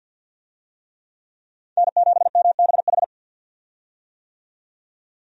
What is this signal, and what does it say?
Morse code, heard as a keyed tone, about 52 wpm, 700 Hz. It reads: N7MBH
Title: USA Call Signs - 5 Characters - 50wpm